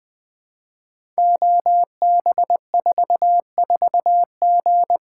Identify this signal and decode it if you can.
Morse code, heard as a keyed tone, about 20 wpm, 700 Hz. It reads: OB44G
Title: Code Groups - 5 Letter:Numbers - 20wpm